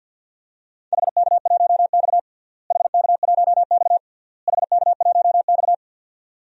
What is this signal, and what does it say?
Morse code, heard as a keyed tone, about 49 wpm, 700 Hz. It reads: HK1X HK1X HK1X
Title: DXCC-100 Call Signs - 4 Characters - Repeated 3x - 50wpm